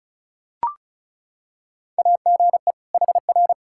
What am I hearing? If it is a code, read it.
Morse code, heard as a keyed tone, about 36 wpm, 700 Hz. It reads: AGE HR
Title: Practice QSO - Exchange 3 - More Info - 35wpm